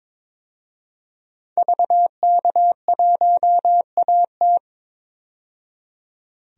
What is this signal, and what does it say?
Morse code, heard as a keyed tone, about 22 wpm, 700 Hz. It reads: VK1AT
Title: DXCC-100 Call Signs - 5 Characters - 22wpm